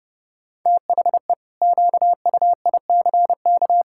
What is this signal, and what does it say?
Morse code, heard as a keyed tone, about 32 wpm, 700 Hz. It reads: THE QUICK
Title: Pangram Challenge v2 - 30wpm